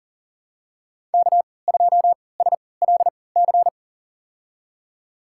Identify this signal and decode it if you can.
Morse code, heard as a keyed tone, about 40 wpm, spaced as for 23 wpm, 700 Hz. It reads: K2SLC
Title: USA Call Signs - 5 Characters - 3x Character Spacing - 40wpm